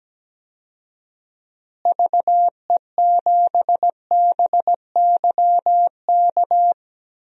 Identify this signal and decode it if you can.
Morse code, heard as a keyed tone, about 17 wpm, 700 Hz. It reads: VE7BYK
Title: DXCC-100 Call Signs - 6 Characters - Repeated 2x - 17wpm